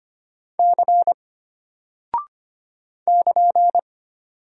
Morse code, heard as a keyed tone, about 25 wpm, 700 Hz. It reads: C <KN>
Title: Single Character (Letter:Number:Punctuation:Prosign) Alt 25wpm